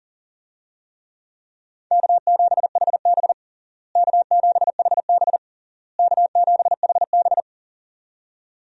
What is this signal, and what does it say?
Morse code, heard as a keyed tone, about 37 wpm, 700 Hz. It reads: K7HB K7HB K7HB
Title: USA Call Signs - 4 Characters - Repeated 3x - 40wpm